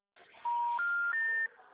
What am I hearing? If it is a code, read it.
Special information tone.
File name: out- -2002- - - .12